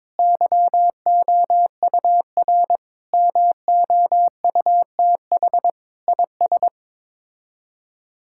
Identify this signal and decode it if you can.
Morse code, heard as a keyed tone, about 22 wpm, 700 Hz. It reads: YOUR MOUTH IS
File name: practice / 22wpm / Sets of 7 Words - 22wpm